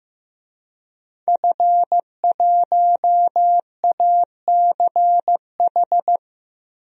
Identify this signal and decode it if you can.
Morse code, heard as a keyed tone, about 15 wpm, 700 Hz. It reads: F1ACH
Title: DXCC-100 Call Signs - 5 Characters - Repeated 2x - 15wpm